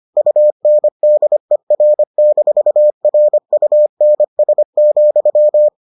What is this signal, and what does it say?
Morse code, heard as a keyed tone, about 25 wpm, 600 Hz. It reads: UNDER-RUNS,